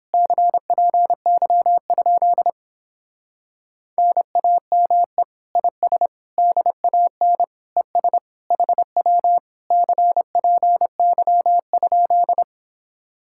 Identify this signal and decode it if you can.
Morse code, heard as a keyed tone, about 31 wpm, 700 Hz. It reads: CPY? NAME IS DAN ES HW CPY?